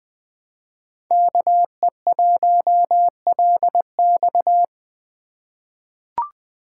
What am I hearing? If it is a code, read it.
Morse code, heard as a keyed tone, about 20 wpm, 700 Hz. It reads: KE1LX
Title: USA Call Signs - 5 Characters - 3x Character Spacing - 20wpm